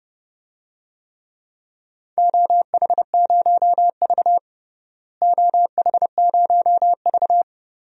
Morse code, heard as a keyed tone, about 32 wpm, 700 Hz. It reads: OH0V OH0V
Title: DXCC-100 Call Signs - 4 Characters - Repeated 2x - 30wpm